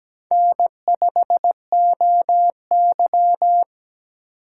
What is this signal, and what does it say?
Morse code, heard as a keyed tone, about 17 wpm, 700 Hz. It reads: N5OY
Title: USA Call Signs - 3x Character Spacing - 17wpm